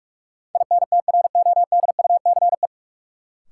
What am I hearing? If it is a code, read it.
Morse code, heard as a keyed tone, about 46 wpm, 700 Hz. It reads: INTRODUCE
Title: Sentences from Top 1600 Words - New Word Drill Alt - 45wpm